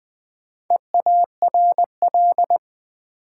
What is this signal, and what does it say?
Morse code, heard as a keyed tone, about 20 wpm, 700 Hz. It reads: EARL
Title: Build A Word - 1 Letter At A Time - 5-Letter Words - 20wpm